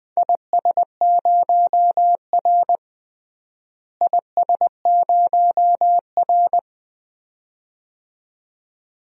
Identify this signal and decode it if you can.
Morse code, heard as a keyed tone, about 20 wpm, 700 Hz. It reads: IS0R IS0R
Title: DXCC-100 Call Signs - 4 Characters - Repeated 2x - 20wpm